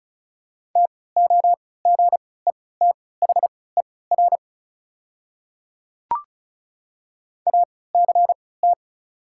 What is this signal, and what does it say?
Morse code, heard as a keyed tone, about 34 wpm, spaced as for 20 wpm, 690 Hz. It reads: TOGETHER ACT
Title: 3x Character Spacing - Top 500 Words - 35wpm